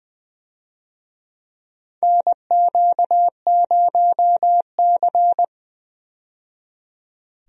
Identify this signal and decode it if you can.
Morse code, heard as a keyed tone, about 20 wpm, 700 Hz. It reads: NQ0C
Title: USA Call Signs - 4 Characters - 20wpm